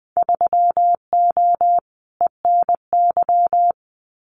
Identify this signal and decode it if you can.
Morse code, heard as a keyed tone, about 20 wpm, 700 Hz. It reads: 3O ENY
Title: Winter Field Day Exchanges Encore 20wpm